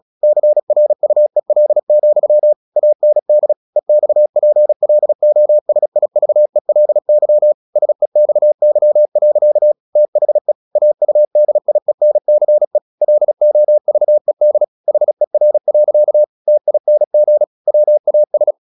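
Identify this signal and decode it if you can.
Morse code, heard as a keyed tone, about 36 wpm, 600 Hz. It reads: CRUEL, AND EXPLOSIVELY SEXY. THE AUDIENCE LOVED HER. TING WAS
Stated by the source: Morse practice at 36 wpm, text of Ken Follett's novel Never, from CW med Gnister.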